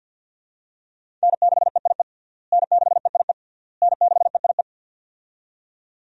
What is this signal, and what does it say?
Morse code, heard as a keyed tone, about 50 wpm, 700 Hz. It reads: N6EIE N6EIE N6EIE
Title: USA Call Signs - 5 Characters - Repeated 3x - 50wpm